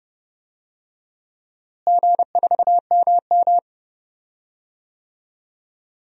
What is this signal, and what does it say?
Morse code, heard as a keyed tone, about 32 wpm, 700 Hz. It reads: G4MM
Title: DXCC-100 Call Signs - 4 Characters - Repeated 2x - 30wpm